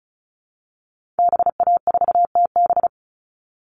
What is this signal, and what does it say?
Morse code, heard as a keyed tone, about 34 wpm, 700 Hz. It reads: BA4TB
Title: Call Signs Repeated 3x - 35wpm